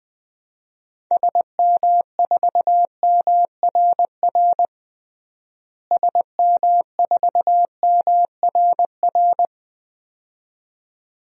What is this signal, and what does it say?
Morse code, heard as a keyed tone, about 21 wpm, 700 Hz. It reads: SM4MRR SM4MRR
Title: DXCC-100 Call Signs - 6 Characters - Repeated 2x - 20wpm